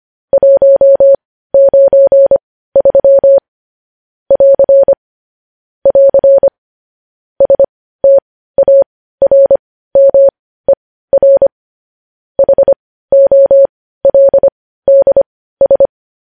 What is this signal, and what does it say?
Morse code, heard as a keyed tone, about 25 wpm, spaced as for 15 wpm, 550 Hz. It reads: 193 <AR> <AR> STARMER HOLDS